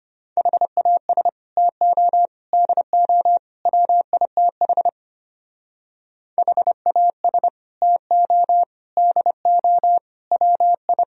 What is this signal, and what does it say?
Morse code, heard as a keyed tone, about 30 wpm, 700 Hz. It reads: HAS TO DO WITH HAS TO DO WI
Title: Speed-Racing - Sets of 4 Words 25wpm